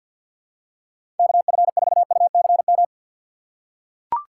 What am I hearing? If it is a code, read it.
Morse code, heard as a keyed tone, about 45 wpm, 700 Hz. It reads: KF4UCM
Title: USA Call Signs - 3x Character Spacing - 50wpm